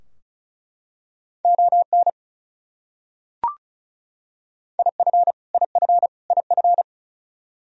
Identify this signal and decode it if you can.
Morse code, heard as a keyed tone, about 34 wpm, 700 Hz. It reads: ON IF IF IF
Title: Top 100 Words Repeated 35wpm